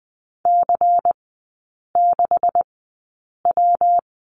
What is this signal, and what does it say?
Morse code, heard as a keyed tone, about 20 wpm, spaced as for 8 wpm, 700 Hz. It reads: C6W
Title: 191 Lesson 31e - Letter v - Three Characters 20wpm